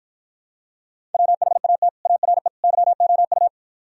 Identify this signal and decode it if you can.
Morse code, heard as a keyed tone, about 54 wpm, 700 Hz. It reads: WHAT ARE YOU